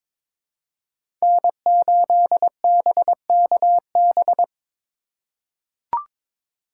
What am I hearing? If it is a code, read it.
Morse code, heard as a keyed tone, about 22 wpm, 700 Hz. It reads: N8BKB
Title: USA Call Signs - 5 Characters - 4x Character Spacing - 22wpm